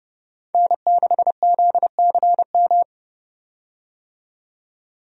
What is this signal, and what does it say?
Morse code, heard as a keyed tone, about 31 wpm, 700 Hz. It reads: N6ZCM
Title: USA Call Signs - 5 Characters - 30wpm